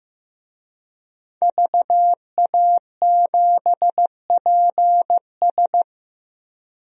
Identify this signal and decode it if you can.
Morse code, heard as a keyed tone, about 15 wpm, 700 Hz. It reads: VA7PS
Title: DXCC-100 Call Signs - 5 Characters - 15wpm